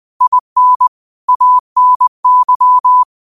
Morse code, heard as a keyed tone, about 20 wpm, 1000 Hz. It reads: IN ANY
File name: qotdfiles / 20wpm